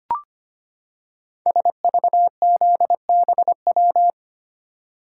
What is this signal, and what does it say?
Morse code, heard as a keyed tone, about 25 wpm, 700 Hz. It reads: SVZBW
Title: Code Groups - 5 Letters - 25wpm